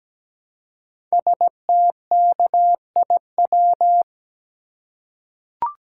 Morse code, heard as a keyed tone, about 17 wpm, 700 Hz. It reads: STKIW
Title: Code Groups - 5 Letters - 17wpm